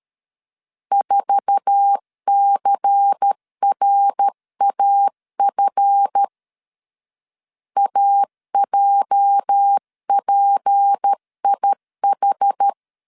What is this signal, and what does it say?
Morse code, heard as a keyed tone, about 13 wpm, 800 Hz. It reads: 4CRAF AJPIH
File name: LEZIONI / 9 286 12wpm 2305